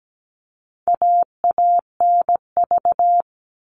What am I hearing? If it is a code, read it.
Morse code, heard as a keyed tone, about 17 wpm, 700 Hz. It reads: AANV
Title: All Cut Numbers - 4-Digit 17wpm